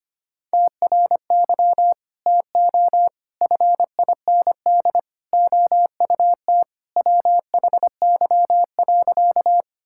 Morse code, heard as a keyed tone, about 25 wpm, 700 Hz. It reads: TRY TO FIND OUT WHY.
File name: Sentences with 3-Word Phrases - 25wpm